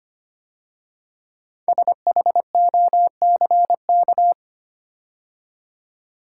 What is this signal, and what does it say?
Morse code, heard as a keyed tone, about 25 wpm, 700 Hz. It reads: SHOCK
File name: 5 Letter Words - First 3 Letters Given - 25wpm